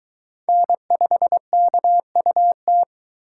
Morse code, heard as a keyed tone, about 21 wpm, 700 Hz. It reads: N5KUT